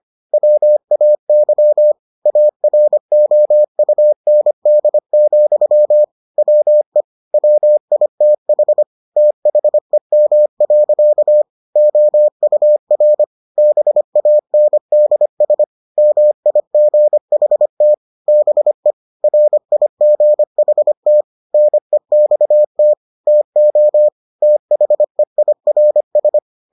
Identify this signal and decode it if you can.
Morse code, heard as a keyed tone, about 25 wpm, 600 Hz. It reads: WAY AROUND, WE WITH THEM. OUR BANDS MIGHT BE RIGHT NEXT TO THEIRS